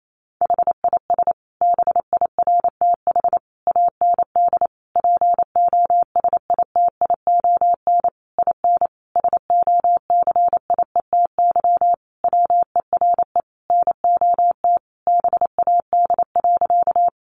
Morse code, heard as a keyed tone, about 28 wpm, 700 Hz. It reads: HIS BIRTH AND POSITION IN SOCIETY WERE NOT BAD.
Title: Sentences from Top 900 Words 28wpm